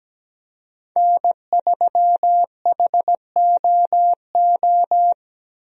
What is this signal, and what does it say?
Morse code, heard as a keyed tone, about 17 wpm, 700 Hz. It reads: N3HOO
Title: USA Call Signs - 5 Characters - 17wpm